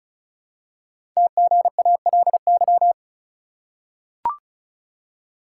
Morse code, heard as a keyed tone, about 36 wpm, 700 Hz. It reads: TGALY
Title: Code Groups - 5 Letters - 35wpm